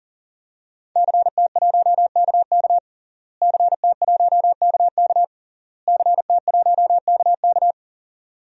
Morse code, heard as a keyed tone, about 40 wpm, 700 Hz. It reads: CT1KK CT1KK CT1KK
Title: DXCC-100 Call Signs - 5 Characters - Repeated 3x - 40wpm